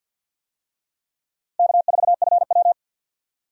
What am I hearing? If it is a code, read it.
Morse code, heard as a keyed tone, about 52 wpm, 700 Hz. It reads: K4FW
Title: USA Call Signs - 4 Characters - Repeated 2x - 50wpm